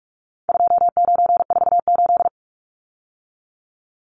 Morse code, heard as a keyed tone, about 44 wpm, 700 Hz. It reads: 2948
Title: Numbers - 4-Digit 45wpm